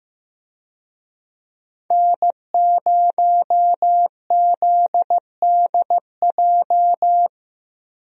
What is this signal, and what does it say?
Morse code, heard as a keyed tone, about 15 wpm, 700 Hz. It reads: N0ZDJ
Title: USA Call Signs - 5 Characters - Repeated 2x - 15wpm